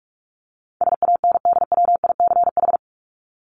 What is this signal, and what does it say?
Morse code, heard as a keyed tone, about 44 wpm, 700 Hz. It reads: SANDWICH
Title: Top 1751-2000 Words 45wpm